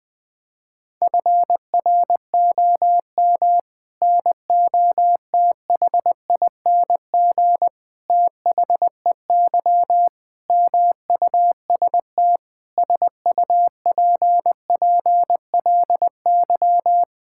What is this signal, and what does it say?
Morse code, heard as a keyed tone, about 20 wpm, 700 Hz. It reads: FROM NOTHING THEY MUST SUPPLY
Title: Sentences from Top 1000 Words - Review Part 5 - 20wpm